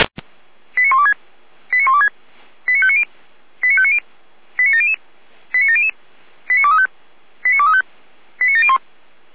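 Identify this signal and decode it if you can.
Five-tone selective call (ZVEI), heard as quick runs of five tones, tones of about 80 ms. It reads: 89126 89126 89590 89590 89700 89700 89235 89235 89701
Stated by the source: ZVEI five-tone selective call received on SDR.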